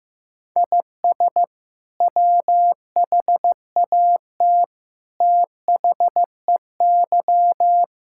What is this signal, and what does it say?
Morse code, heard as a keyed tone, about 15 wpm, 700 Hz. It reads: IS WHAT THEY
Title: Sentences from Top 1000 Words - Review Part 5 - 15wpm